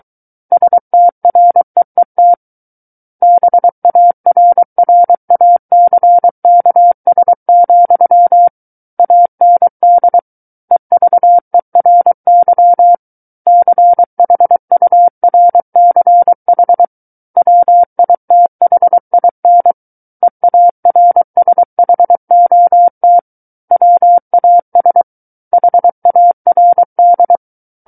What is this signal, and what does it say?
Morse code, heard as a keyed tone, about 23 wpm, 700 Hz. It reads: STREET BARRACKS, AND EVERY CHURCH WITHIN EARSHOT WAS HARD
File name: Book One / 14-Chapter 14 - 23 WPM